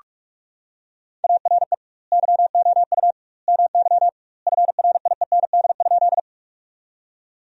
Morse code, heard as a keyed tone, about 46 wpm, 700 Hz. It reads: ARE YOU MY FRIEND?
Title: 2x Word-Spacing - Sentences from Top 300 Words - 45wpm